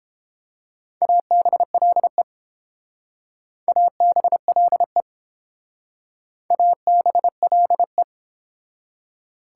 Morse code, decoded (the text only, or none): ABLE ABLE ABLE